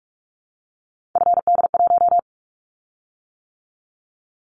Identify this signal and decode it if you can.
Morse code, heard as a keyed tone, about 46 wpm, 700 Hz. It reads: FD1
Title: Callsign Prefixes for Top 25 DXCC - 45wpm